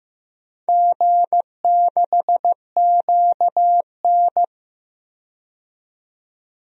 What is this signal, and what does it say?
Morse code, heard as a keyed tone, about 15 wpm, 700 Hz. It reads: G6QN
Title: DXCC-100 Call Signs - 4 Characters - 15wpm